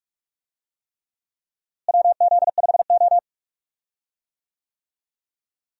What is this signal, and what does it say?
Morse code, heard as a keyed tone, about 48 wpm, 700 Hz. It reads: WZ5O
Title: USA Call Signs - 4 Characters - Repeated 3x - 45wpm